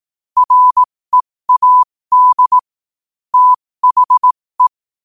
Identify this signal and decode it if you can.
Morse code, heard as a keyed tone, about 18 wpm, spaced as for 15 wpm, 1000 Hz. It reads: READ THE
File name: qotdfiles / 15wpm